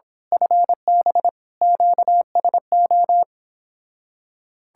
Morse code, heard as a keyed tone, about 25 wpm, 700 Hz. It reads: FB QSO
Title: Speed-Racing Common QSO Phrases - 22wpm